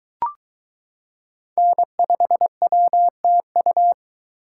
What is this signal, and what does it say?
Morse code, heard as a keyed tone, about 22 wpm, 700 Hz. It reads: N5WTU